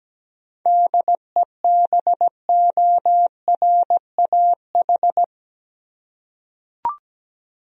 Morse code, heard as a keyed tone, about 17 wpm, 700 Hz. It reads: DEBORAH
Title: USA Common Names - 17wpm